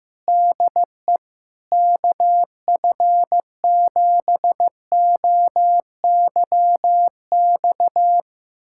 Morse code, heard as a keyed tone, about 15 wpm, 700 Hz. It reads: DE KF7OYX
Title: USA Callsign DE Callsign - 6 Characters - 15wpm